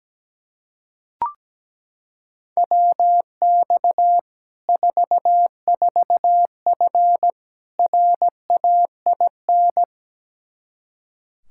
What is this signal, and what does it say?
Morse code, heard as a keyed tone, about 17 wpm, 700 Hz. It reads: WX 44F RAIN
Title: Practice QSO - Exchange 2 - Info - 17wpm